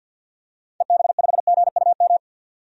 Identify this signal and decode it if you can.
Morse code, heard as a keyed tone, about 53 wpm, 700 Hz. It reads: EB5GUM
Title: DXCC-100 Call Signs - 6 Characters - 50wpm